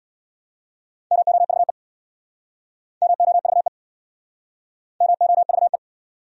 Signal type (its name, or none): Morse code